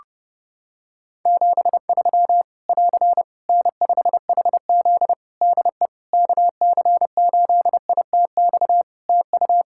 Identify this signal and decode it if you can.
Morse code, heard as a keyed tone, about 30 wpm, 700 Hz. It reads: 73 <AR> N5HZ DE KC8ITX TU
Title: Practice QSO - Exchange 4 - Ending - 30wpm